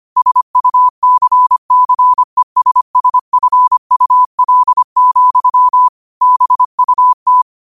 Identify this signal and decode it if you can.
Morse code, heard as a keyed tone, about 25 wpm, 1000 Hz. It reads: SUCCESSFUL, BUT
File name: qotdfiles / 25wpm